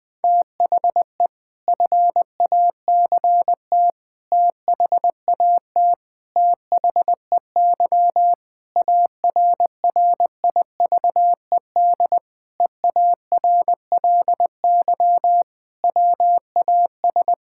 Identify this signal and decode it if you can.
Morse code, heard as a keyed tone, about 20 wpm, 700 Hz. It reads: THE FACT THAT THEY ARRIVED EARLY WAS
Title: Sentences with 4-Word Phrases - 20wpm